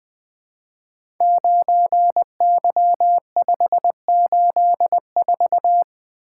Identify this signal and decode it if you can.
Morse code, heard as a keyed tone, about 20 wpm, 700 Hz. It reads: 9Y584
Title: Code Groups - 5 Letter:Numbers - 20wpm